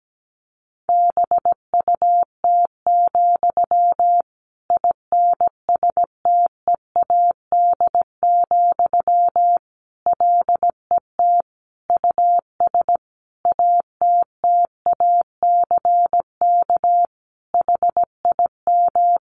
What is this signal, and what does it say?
Morse code, heard as a keyed tone, about 17 wpm, 700 Hz. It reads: BUT, INSTEAD, LET US ATTACK HIM
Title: Sentences from Top 1500 Words 17wpm